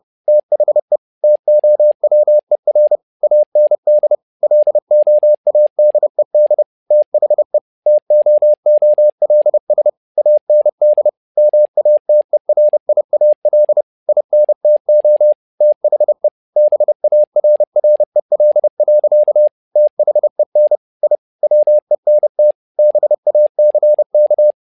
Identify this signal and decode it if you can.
Morse code, heard as a keyed tone, about 30 wpm, 600 Hz. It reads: THE TOWER AND LOADED THE TOOLS AND MATERIAL INTO THE BARREL. THEN I WENT BACK
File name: learning center / tower trouble / tower trouble 30wpm